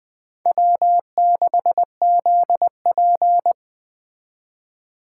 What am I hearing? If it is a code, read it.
Morse code, heard as a keyed tone, about 20 wpm, 700 Hz. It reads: W6ZP